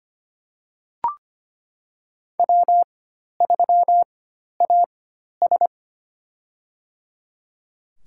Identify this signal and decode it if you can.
Morse code, heard as a keyed tone, about 25 wpm, spaced as for 12 wpm, 700 Hz. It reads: W3AS
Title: USA Call Signs - 4 Characters - 4x Character Spacing - 25wpm